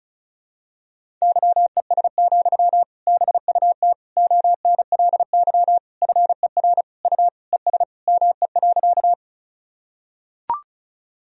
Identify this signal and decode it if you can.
Morse code, heard as a keyed tone, about 36 wpm, 700 Hz. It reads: YES, BUT ONLY FER U ES ME.
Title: Sentences from Top 500 Words and Abbreviations 35wpm